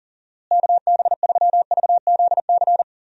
Morse code, heard as a keyed tone, about 40 wpm, 700 Hz. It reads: KB3VZC
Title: USA Call Signs - 6 Characters - Repeated 3x - 40wpm